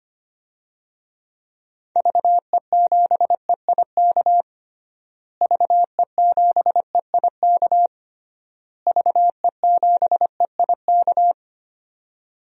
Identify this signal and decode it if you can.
Morse code, heard as a keyed tone, about 25 wpm, 700 Hz. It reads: VE7EIK VE7EIK VE7EIK
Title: DXCC-100 Call Signs - 6 Characters - Repeated 3x - 25wpm